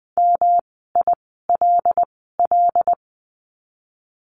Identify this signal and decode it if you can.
Morse code, heard as a keyed tone, about 20 wpm, spaced as for 14 wpm, 700 Hz. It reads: MILL
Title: 108 Lesson 18f - Letter m - Words 20wpm